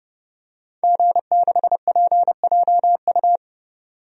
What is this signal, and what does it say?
Morse code, heard as a keyed tone, about 30 wpm, 700 Hz. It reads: G6PJU
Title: DXCC-100 Call Signs - 5 Characters - 30wpm